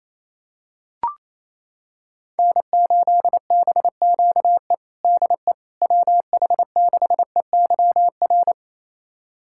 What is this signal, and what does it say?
Morse code, heard as a keyed tone, about 28 wpm, 700 Hz. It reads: N8BQE DE WH6EYR
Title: USA Callsign DE Callsign 28wpm